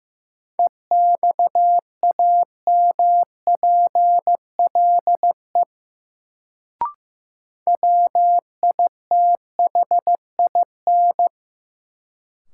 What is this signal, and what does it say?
Morse code, heard as a keyed tone, about 15 wpm, 700 Hz. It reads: EXAMPLE WITHIN
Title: Sets of 2 Words - New Word Drill - 15wpm